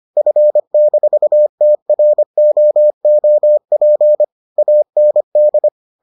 Morse code, heard as a keyed tone, about 25 wpm, 600 Hz. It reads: F-TROOP AND